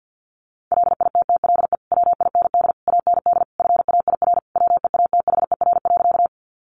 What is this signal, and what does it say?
Morse code, heard as a keyed tone, about 50 wpm, 700 Hz. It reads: LITTLE WIND AND FAIR WEATHER.
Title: Sentences from Top 1300 Words 50wpm